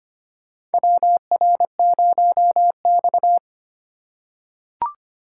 Morse code, heard as a keyed tone, about 25 wpm, 700 Hz. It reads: WR0X
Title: USA Call Signs - 4 Characters - 2x Character Spacing - 25wpm